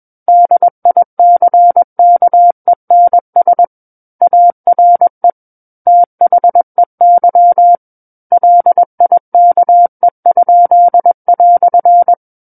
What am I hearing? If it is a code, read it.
Morse code, heard as a keyed tone, about 21 wpm, 700 Hz. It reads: DICKENS ARE THEY LIKE?"
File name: Book One / 12-Chapter 12 - 21 WPM